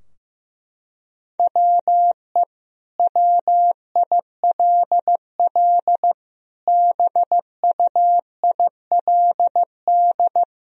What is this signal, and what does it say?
Morse code, heard as a keyed tone, about 15 wpm, 700 Hz. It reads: WE WILL BUILD
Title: Sentences from Top 1000 Words - Review Part 4 - 15wpm